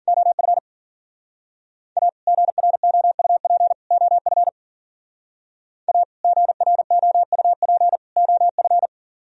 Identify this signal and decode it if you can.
Morse code, heard as a keyed tone, about 51 wpm, 700 Hz. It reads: OF A GROUP OF A GROUP OF